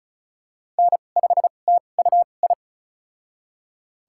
Morse code, heard as a keyed tone, about 36 wpm, spaced as for 25 wpm, 700 Hz. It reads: N5TUI
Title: USA Call Signs - 2x Character Spacing - 35wpm